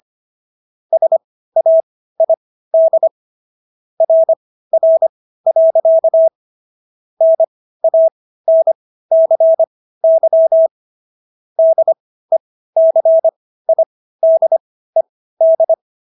Morse code, heard as a keyed tone, about 25 wpm, spaced as for 15 wpm, 650 Hz. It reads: SAID RR. NANCY DECIDED